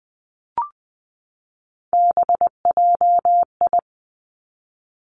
Morse code, heard as a keyed tone, about 20 wpm, 700 Hz. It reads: BJI
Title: Three Challenging Letter-Number Combinations 20wpm